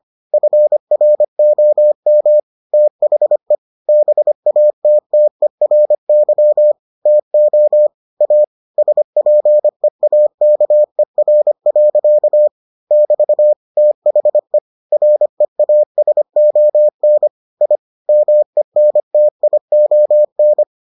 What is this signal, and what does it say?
Morse code, heard as a keyed tone, about 25 wpm, 600 Hz. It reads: FROM THE BATTERY TO A SPEAKER. <BT> THE REASON I MENTION